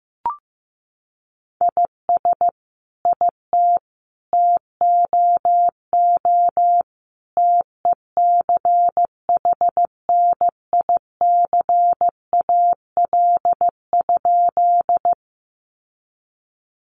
Morse code, heard as a keyed tone, about 15 wpm, 700 Hz. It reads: IS IT TOO TECHNICAL?
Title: Sentences from Top 1100 Words 15wpm